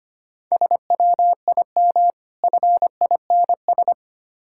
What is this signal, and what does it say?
Morse code, heard as a keyed tone, about 25 wpm, 690 Hz. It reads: SWIM FINS
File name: Easy Words - Speed-Racing with Hints - 25wpm